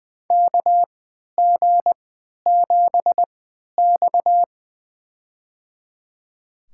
Morse code, heard as a keyed tone, about 20 wpm, spaced as for 11 wpm, 700 Hz. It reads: KG7X
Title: USA Call Signs - 4 Characters - 3x Character Spacing - 20wpm